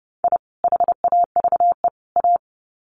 Morse code, heard as a keyed tone, about 30 wpm, 700 Hz. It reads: I HAVE A
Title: Sets of 3 Words 30wpm